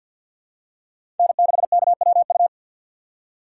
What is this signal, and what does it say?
Morse code, heard as a keyed tone, about 49 wpm, 700 Hz. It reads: N6KWU
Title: USA Call Signs - 5 Characters - 2x Character Spacing - 50wpm